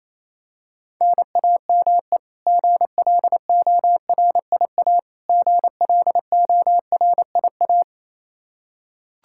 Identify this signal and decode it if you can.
Morse code, heard as a keyed tone, about 27 wpm, 700 Hz. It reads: NAME GLORIA GLORIA
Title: K1USN SST QSO Practice 28wpm